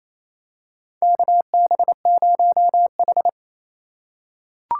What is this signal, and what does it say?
Morse code, heard as a keyed tone, about 28 wpm, 700 Hz. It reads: KB0H
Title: USA Call Signs - 3x Character Spacing - 28wpm